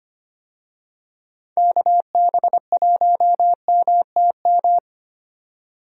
Morse code, heard as a keyed tone, about 25 wpm, 700 Hz. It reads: KB1MTM
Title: USA Call Signs - 6 Characters - 25wpm